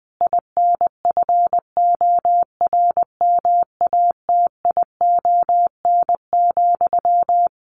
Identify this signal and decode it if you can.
Morse code, heard as a keyed tone, about 20 wpm, 700 Hz. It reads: INFORMATION,
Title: Sentences from Top 600 Words 20wpm